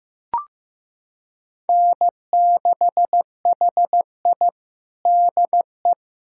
Morse code, heard as a keyed tone, about 15 wpm, 700 Hz. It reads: N6HI DE
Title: USA Callsign DE Callsign - 4 Characters - 15wpm